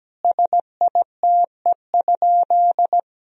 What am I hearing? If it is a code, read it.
Morse code, heard as a keyed tone, about 17 wpm, 700 Hz. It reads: SITE?
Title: Sentences from Top 1000 Words - Review Part 4 - 17wpm